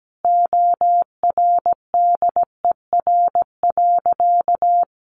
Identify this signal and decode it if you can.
Morse code, heard as a keyed tone, about 17 wpm, 700 Hz. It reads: ORDER.
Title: Sentences from Top 1300 Words 17wpm